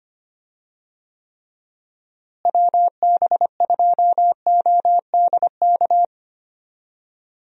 Morse code, heard as a keyed tone, about 25 wpm, 700 Hz. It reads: WB2ODK
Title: USA Call Signs - 6 Characters - Repeated 2x - 25wpm